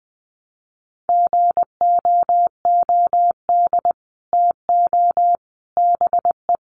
Morse code, heard as a keyed tone, about 20 wpm, 700 Hz. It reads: GOOD TO BE
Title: Sets of 5 Words Encore2 20wpm